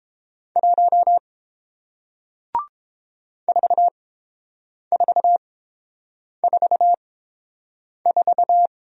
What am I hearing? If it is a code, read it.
Morse code, heard as a keyed tone, about 29 wpm, spaced as for 7 wpm, 700 Hz. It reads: 1 4444